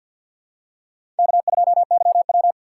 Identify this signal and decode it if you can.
Morse code, heard as a keyed tone, about 46 wpm, 700 Hz. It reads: K2YW
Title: USA Call Signs - 4 Characters - 2x Character Spacing - 50wpm